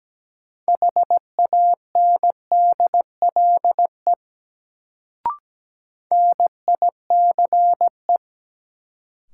Morse code, heard as a keyed tone, about 17 wpm, 700 Hz. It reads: HANDLE NICE
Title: Sentences from Top 1100 Words - New Word Drill - 17wpm